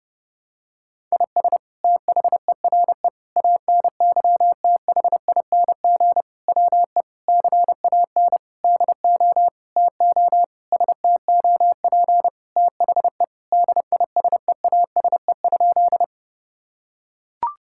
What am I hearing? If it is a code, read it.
Morse code, heard as a keyed tone, about 30 wpm, 700 Hz. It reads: IS THERE ANYTHING WE CAN DO TO STOP THE DISEASE?
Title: Sentences from Top 1000 Words - Review Part 6 - 30wpm